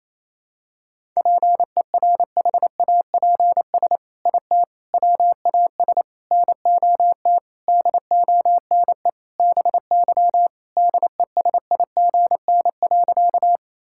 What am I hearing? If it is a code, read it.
Morse code, heard as a keyed tone, about 28 wpm, 700 Hz. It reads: PERHAPS IT WAS NOT DONE BY DESIGN.
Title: Sentences from Top 1000 Words - Review Part 5 - 28wpm